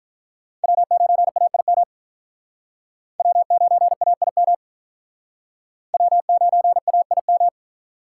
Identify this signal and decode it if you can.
Morse code, heard as a keyed tone, about 46 wpm, 700 Hz. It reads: W9AIM W9AIM W9AIM